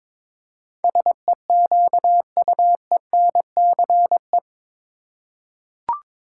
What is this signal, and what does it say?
Morse code, heard as a keyed tone, about 22 wpm, 700 Hz. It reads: SEQUENCE
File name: practice / 22wpm / Eight Letter Words - 22wpm